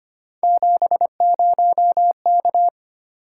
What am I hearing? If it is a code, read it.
Morse code, heard as a keyed tone, about 25 wpm, 700 Hz. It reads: 70K
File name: Three Letter-Number Combinations - 25wpm